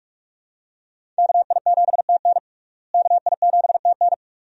Morse code, heard as a keyed tone, about 46 wpm, 700 Hz. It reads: KI7TN KI7TN
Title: USA Call Signs - 5 Characters - Repeated 2x - 45wpm